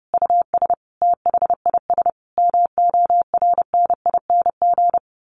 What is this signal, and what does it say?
Morse code, heard as a keyed tone, about 30 wpm, 700 Hz. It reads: US THIS MORNING
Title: Sets of 5 Words Encore 30wpm